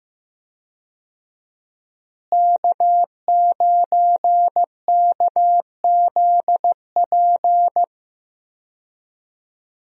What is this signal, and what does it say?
Morse code, heard as a keyed tone, about 15 wpm, 700 Hz. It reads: K9KZP
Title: USA Call Signs - 5 Characters - Repeated 3x - 15wpm